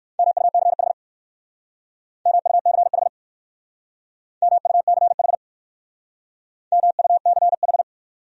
Morse code, heard as a keyed tone, about 61 wpm, 700 Hz. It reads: MUCH MUCH MUCH MUCH